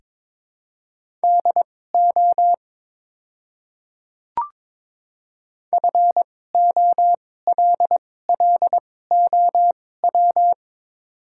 Morse code, heard as a keyed tone, about 22 wpm, spaced as for 16 wpm, 700 Hz. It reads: DO FOLLOW
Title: 2x Character Spacing - Top 300 Words - 22wpm